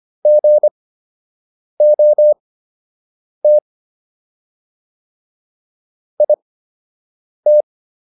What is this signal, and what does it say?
Morse code, heard as a keyed tone, about 25 wpm, spaced as for 7 wpm, 600 Hz. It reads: GOT IT